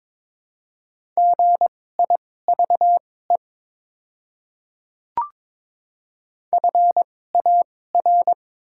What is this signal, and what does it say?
Morse code, heard as a keyed tone, about 22 wpm, spaced as for 16 wpm, 700 Hz. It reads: GIVE FAR
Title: 2x Character Spacing - Top 300 Words - 22wpm